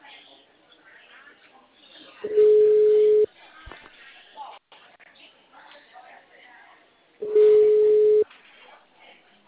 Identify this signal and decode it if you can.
Ringback tone (Europe, 425 Hz), with speech.
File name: out- -2030- - - .87